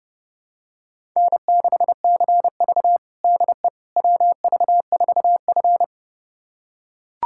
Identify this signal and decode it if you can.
Morse code, heard as a keyed tone, about 30 wpm, 700 Hz. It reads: N6CV DE WV4F
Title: USA Callsign DE Callsign - 4 Characters - 30wpm